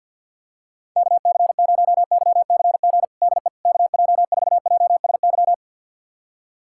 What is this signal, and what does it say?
Morse code, heard as a keyed tone, about 50 wpm, 700 Hz. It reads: KC0YCG DE KJ4JSY
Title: USA Callsign DE Callsign - 6 Characters - 50wpm